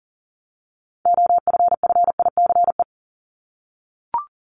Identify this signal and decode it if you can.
Morse code, heard as a keyed tone, about 40 wpm, 700 Hz. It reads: OFFICE
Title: Top 751-1000 Words 40wpm